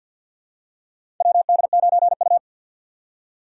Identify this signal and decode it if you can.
Morse code, heard as a keyed tone, about 48 wpm, 700 Hz. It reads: WD9U
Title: USA Call Signs - 2x Character Spacing - 50wpm